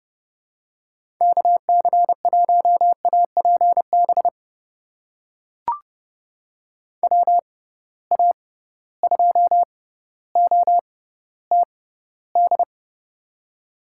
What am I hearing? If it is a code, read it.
Morse code, heard as a keyed tone, about 30 wpm, spaced as for 15 wpm, 700 Hz. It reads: KC1APB WA2OTD
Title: USA Call Signs - 6 Characters - 6x Character Spacing - 30wpm